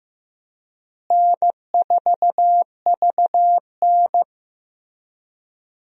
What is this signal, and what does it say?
Morse code, heard as a keyed tone, about 16 wpm, 700 Hz. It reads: N4VN
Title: USA Call Signs - 4 Characters - Repeated 2x - 15wpm